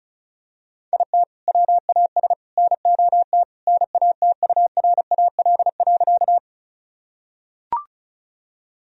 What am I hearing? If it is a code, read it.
Morse code, heard as a keyed tone, about 34 wpm, 700 Hz. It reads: IT WAS NOT NATURAL.
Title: Sentences from Top 1000 Words - Review Part 4 - 35wpm